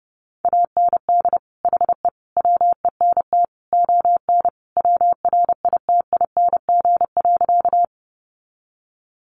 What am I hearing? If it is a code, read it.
Morse code, heard as a keyed tone, about 30 wpm, 700 Hz. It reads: AND HE WENT ON WRITING.
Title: Sentences from Top 800 Words 30wpm